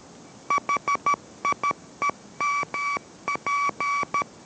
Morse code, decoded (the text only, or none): HIEMP